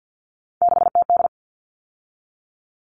Morse code, heard as a keyed tone, about 50 wpm, 700 Hz. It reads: 6TD
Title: All Cut Numbers - 3-Digit 50wpm